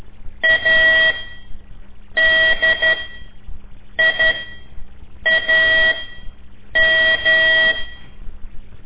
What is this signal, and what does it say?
Morse code, heard as a keyed tone, about 12 wpm, spaced as for 6 wpm, 1950 Hz. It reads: ADIAM
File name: ik1weg2 / CORSI CW / lez5-1